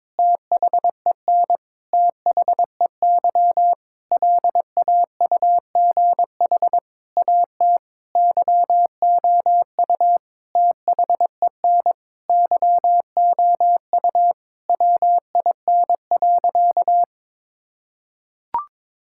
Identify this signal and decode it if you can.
Morse code, heard as a keyed tone, about 22 wpm, 700 Hz. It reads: THEN THEY LAUGH AT YOU THEN YOU WIN.
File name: practice / 22wpm / Sentences with 8-Word Phrases - 22wpm